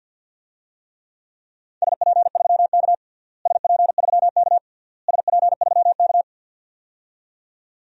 Morse code, heard as a keyed tone, about 53 wpm, 700 Hz. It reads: SP3K SP3K SP3K
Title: DXCC-100 Call Signs - 4 Characters - Repeated 3x - 50wpm